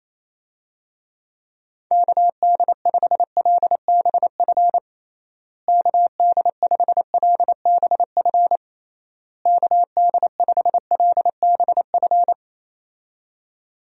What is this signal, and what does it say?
Morse code, heard as a keyed tone, about 27 wpm, 700 Hz. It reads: KD5LBF KD5LBF KD5LBF
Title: USA Call Signs - 6 Characters - Repeated 3x - 28wpm